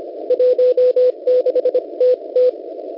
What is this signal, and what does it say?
Morse code, heard as a keyed tone, about 25 wpm, spaced as for 22 wpm, 500 Hz. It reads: 16TT
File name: Unid- -0700utc-4735khz